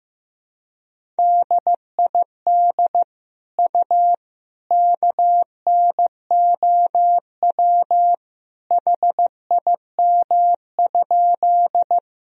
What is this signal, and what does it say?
Morse code, heard as a keyed tone, about 15 wpm, 700 Hz. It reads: DID U KNOW HIM?
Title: Sentences from Top 100 Words and Abbreviations 15wpm